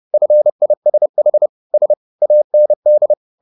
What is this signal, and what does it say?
Morse code, heard as a keyed tone, about 30 wpm, 600 Hz. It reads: FISH S AND